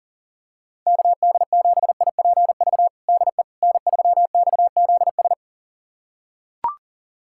Morse code, heard as a keyed tone, about 39 wpm, 700 Hz. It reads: KD7IPV DE N3XZS
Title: USA Callsign DE Callsign 40wpm